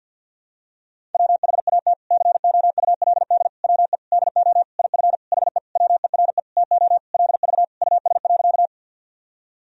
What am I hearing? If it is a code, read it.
Morse code, heard as a keyed tone, about 52 wpm, 700 Hz. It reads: WHAT COULD WE DO IF HE WERE TO LV US.
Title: Sentences from Top 400 Words and Abbreviations 50wpm